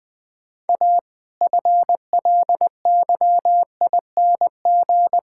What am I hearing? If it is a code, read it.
Morse code, heard as a keyed tone, about 20 wpm, 700 Hz. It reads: A FLYING
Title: All-Age Appropriate Jokes - 20wpm